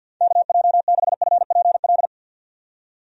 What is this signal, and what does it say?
Morse code, heard as a keyed tone, about 48 wpm, 700 Hz. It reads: KJ6FPL
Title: USA Call Signs - 6 Characters - 50wpm